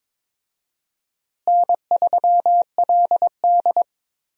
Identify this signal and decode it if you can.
Morse code, heard as a keyed tone, about 22 wpm, 700 Hz. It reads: N3LD
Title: USA Call Signs - 4 Characters - Repeated 2x - 22wpm